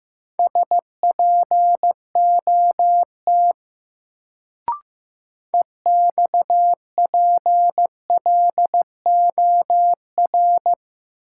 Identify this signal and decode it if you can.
Morse code, heard as a keyed tone, about 15 wpm, 700 Hz. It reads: SPOT EXPLOR
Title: Sentences from Top 1100 Words - New Word Drill Alt - 15wpm